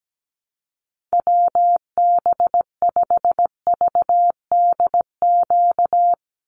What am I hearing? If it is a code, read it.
Morse code, heard as a keyed tone, about 17 wpm, 700 Hz. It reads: WB5VDQ
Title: USA Call Signs 17wpm